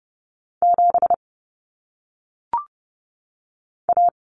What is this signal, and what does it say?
Morse code, heard as a keyed tone, about 30 wpm, 700 Hz. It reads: 7 A
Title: Common Cut Numbers - 1-Digit 30wpm